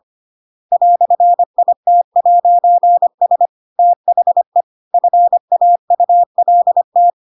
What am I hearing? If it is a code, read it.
Morse code, heard as a keyed tone, about 25 wpm, 700 Hz. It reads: "IT'S THE FAULT
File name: Pinocchio / Pinocchio ch2 25 25 ik1hge